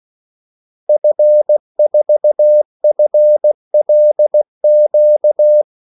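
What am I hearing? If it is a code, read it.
Morse code, heard as a keyed tone, about 16 wpm, 600 Hz. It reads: F4FLQ
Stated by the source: Morse call sign practice at 16 wpm.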